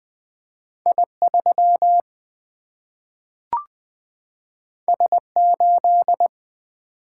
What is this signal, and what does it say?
Morse code, heard as a keyed tone, about 20 wpm, 700 Hz. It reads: I3 S8
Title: Single-Letter Single-Number 20wpm